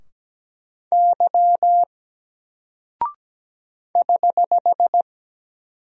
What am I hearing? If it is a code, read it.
Morse code, heard as a keyed tone, about 17 wpm, 700 Hz. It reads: Y <HH>
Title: Single Character (Letter:Number:Punctuation:Prosign) 17wpm